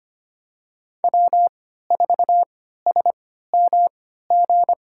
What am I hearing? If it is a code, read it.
Morse code, heard as a keyed tone, about 25 wpm, spaced as for 14 wpm, 700 Hz. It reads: W4SMG
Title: USA Call Signs - 5 Characters - 3x Character Spacing - 25wpm